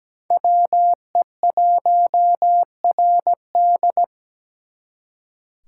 Morse code, heard as a keyed tone, about 20 wpm, 700 Hz. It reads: WE1RD
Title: Speed Racing - USA 2x2 Call Signs 17wpm